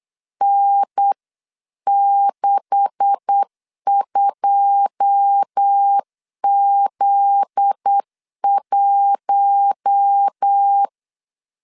Morse code, heard as a keyed tone, about 8 wpm, 800 Hz. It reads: N62Z1
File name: 8wpm